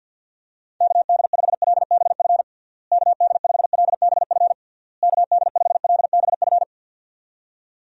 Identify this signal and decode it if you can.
Morse code, heard as a keyed tone, about 51 wpm, 700 Hz. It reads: KD5LBF KD5LBF KD5LBF
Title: USA Call Signs - 6 Characters - Repeated 3x - 50wpm